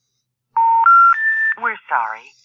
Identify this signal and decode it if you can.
Special information tone, with speech.